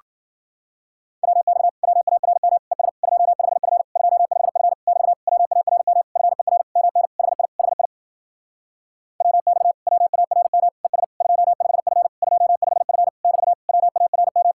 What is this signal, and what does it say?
Morse code, heard as a keyed tone, about 52 wpm, 700 Hz. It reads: WX WARM ES 25F 25F <BT> WARM FER MN HI HI WX WARM ES 25F 25F <BT> WARM